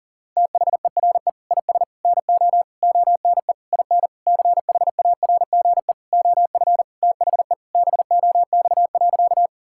Morse code, heard as a keyed tone, about 40 wpm, 700 Hz. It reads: THERE IS NO ONE IN CHARGE OF THE BOX.
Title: Sentences from Top 1000 Words - Review Part 6 - 40wpm